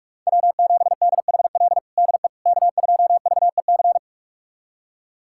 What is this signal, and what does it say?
Morse code, heard as a keyed tone, about 46 wpm, 700 Hz. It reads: W7DHL DE K2VEC
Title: USA Callsign DE Callsign - 5 Characters - 45wpm